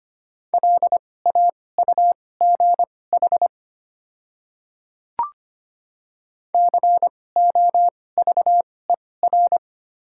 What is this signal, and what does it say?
Morse code, heard as a keyed tone, about 25 wpm, spaced as for 18 wpm, 690 Hz. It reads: LAUGH COVER
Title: 2x Character Spacing - Top 500 Words - 25wpm